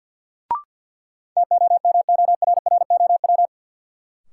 Morse code, heard as a keyed tone, about 52 wpm, 700 Hz. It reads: TOMORROW
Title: Sentences from Top 1100 Words - New Word Drill - 50wpm